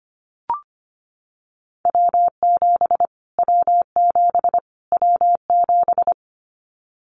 Morse code, heard as a keyed tone, about 25 wpm, 700 Hz. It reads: W7 W7 W7
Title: Callsign Prefixes for Top 25 DXCC - Repeated 3x - 25wpm